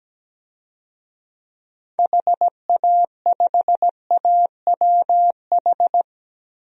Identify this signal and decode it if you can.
Morse code, heard as a keyed tone, about 17 wpm, 700 Hz. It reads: HA5AWH
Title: DXCC-100 Call Signs - 6 Characters - Repeated 3x - 17wpm